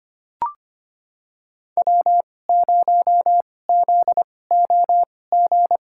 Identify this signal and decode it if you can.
Morse code, heard as a keyed tone, about 25 wpm, spaced as for 18 wpm, 700 Hz. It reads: W0ZOG
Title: USA Call Signs - 5 Characters - 2x Character Spacing - 25wpm